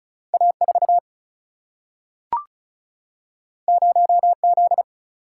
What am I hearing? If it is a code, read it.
Morse code, heard as a keyed tone, about 36 wpm, 700 Hz. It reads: A4 0Z
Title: Single-Letter Single-Number 35wpm